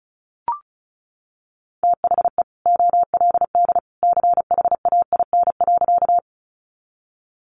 Morse code, heard as a keyed tone, about 34 wpm, 700 Hz. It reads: THE OLD CHAIN.
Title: Sentences from Top 1500 Words 35wpm